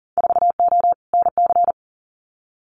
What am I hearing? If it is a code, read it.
Morse code, heard as a keyed tone, about 40 wpm, 700 Hz. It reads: 4O NC
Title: Winter Field Day Exchanges Encore 40wpm